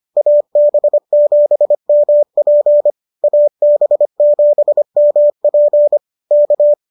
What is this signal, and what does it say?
Morse code, heard as a keyed tone, about 25 wpm, 590 Hz. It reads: AB7MP AB7MP K
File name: qso206 25